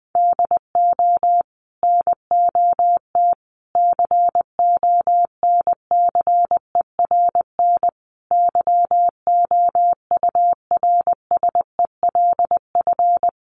Morse code, heard as a keyed tone, about 20 wpm, 700 Hz. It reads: DO NOT CONCERN YOURSELF
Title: Sentences from Top 1000 Words 20wpm